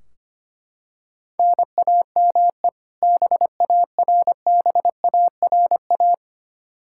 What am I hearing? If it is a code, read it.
Morse code, heard as a keyed tone, about 25 wpm, 700 Hz. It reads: NAME BARBARA
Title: Practice QSO - Exchange 1 - Beginning - 25wpm